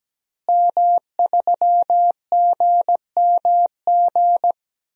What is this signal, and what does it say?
Morse code, heard as a keyed tone, about 17 wpm, 700 Hz. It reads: M3GMG
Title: DXCC-100 Call Signs - 5 Characters - Repeated 3x - 17wpm